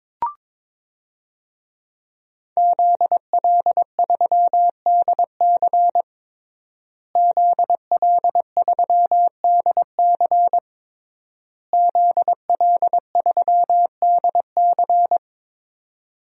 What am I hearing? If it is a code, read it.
Morse code, heard as a keyed tone, about 22 wpm, 700 Hz. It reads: ZL3DC ZL3DC ZL3DC
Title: DXCC-100 Call Signs - 5 Characters - Repeated 3x - 22wpm